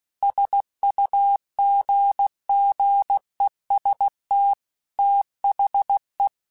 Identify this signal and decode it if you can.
Morse code, heard as a keyed tone, about 16 wpm, 800 Hz. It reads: SUGGEST THE
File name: Technical Subjects / Technical Subjects-mp3s / OR Gate @15